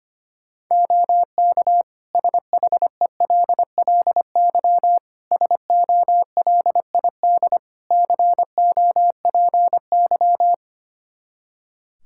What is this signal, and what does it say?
Morse code, heard as a keyed tone, about 25 wpm, 700 Hz. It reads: OK SHELLY SOLID COPY
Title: Practice QSO - Exchange 3 - More Info - 25wpm